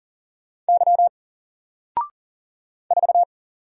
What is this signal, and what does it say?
Morse code, heard as a keyed tone, about 40 wpm, 700 Hz. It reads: Y 4
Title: Single Character (Letter:Number:Punctuation:Prosign) Alt 40wpm